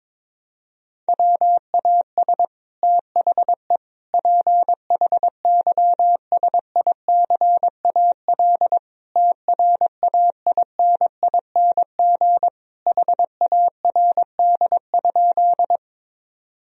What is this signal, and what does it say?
Morse code, heard as a keyed tone, about 22 wpm, 700 Hz. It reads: WAS THE PHYSICAL TRAINING HARD?
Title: Sentences from Top 1000 Words - Review Part 4 - 22wpm